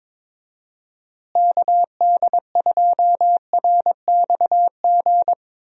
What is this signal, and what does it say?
Morse code, heard as a keyed tone, about 22 wpm, 700 Hz. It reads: KD2RXG
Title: USA Call Signs - 6 Characters - Repeated 2x - 22wpm